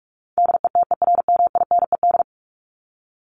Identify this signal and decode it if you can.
Morse code, heard as a keyed tone, about 44 wpm, 700 Hz. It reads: DETERMINED
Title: ED Suffix - Top 100 Words with Base Words 45wpm